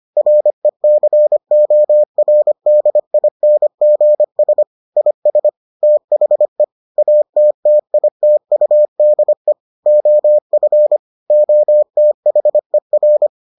Morse code, heard as a keyed tone, about 25 wpm, 600 Hz. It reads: RECORDINGS IS THE ATTITUDE OF OTHER